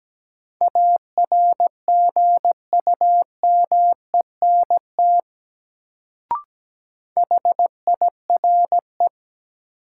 Morse code, heard as a keyed tone, about 17 wpm, 700 Hz. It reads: ARGUMENT HIRE
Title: Sentences from Top 1400 Words - New Word Drill - 17wpm